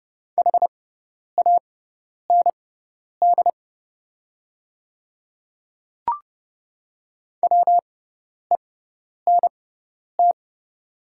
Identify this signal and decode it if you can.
Morse code, heard as a keyed tone, about 30 wpm, spaced as for 10 wpm, 700 Hz. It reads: HAND WENT
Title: 6x Character Spacing - Top 200 Words - 30wpm